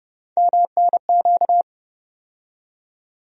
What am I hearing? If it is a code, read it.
Morse code, heard as a keyed tone, about 31 wpm, 700 Hz. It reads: MNQ
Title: USA Call Sign Postfixes - 3 Letters 30wpm